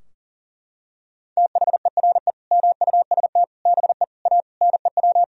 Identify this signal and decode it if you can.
Morse code, heard as a keyed tone, about 40 wpm, 700 Hz. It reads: THERE MUST BE A NEW
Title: Sentences from Top 1000 Words - Review Part 3 - 40wpm